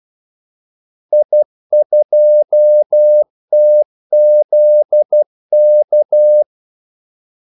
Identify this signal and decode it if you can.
Morse code, heard as a keyed tone, about 12 wpm, 600 Hz. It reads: I2TZK
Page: 2 ekstra ordmellemrum mellem kaldesignalerne